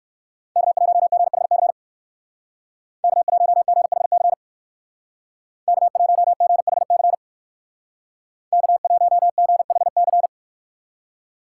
Morse code, heard as a keyed tone, about 59 wpm, 700 Hz. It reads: K1GHC K1GHC K1GHC K1GHC